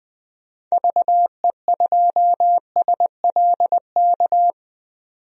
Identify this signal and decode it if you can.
Morse code, heard as a keyed tone, about 20 wpm, 700 Hz. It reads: VE2SLK
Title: DXCC-100 Call Signs - 6 Characters - 20wpm